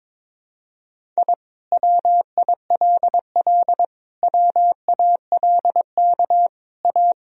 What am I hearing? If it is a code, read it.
Morse code, heard as a keyed tone, about 22 wpm, 700 Hz. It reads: I WILL WALK A
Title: Sentences from Top 1000 Words - Review Part 1 - 22wpm